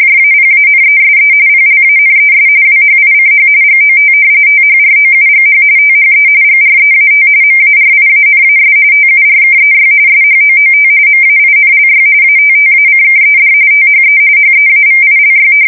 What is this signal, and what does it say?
RTTY, about 45.45 baud. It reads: JAPANESE PLANES WHICH GOT THROUGH THE GREAT MARIANAS TURKEY SHOOT MAINLY ATTACKED THE HEAVILY
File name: fsk170 / battleshipnj